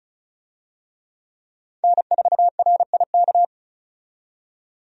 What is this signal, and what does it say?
Morse code, heard as a keyed tone, about 36 wpm, 700 Hz. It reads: N4RIK
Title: USA Call Signs - 5 Characters - 35wpm